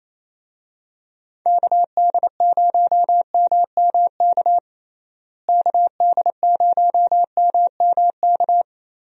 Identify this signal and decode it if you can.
Morse code, heard as a keyed tone, about 28 wpm, 700 Hz. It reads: KD0MMK KD0MMK
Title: USA Call Signs - 6 Characters - Repeated 3x - 28wpm